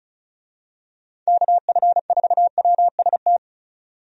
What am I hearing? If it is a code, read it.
Morse code, heard as a keyed tone, about 36 wpm, 700 Hz. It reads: KF4WST
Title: USA Call Signs - 6 Characters - 35wpm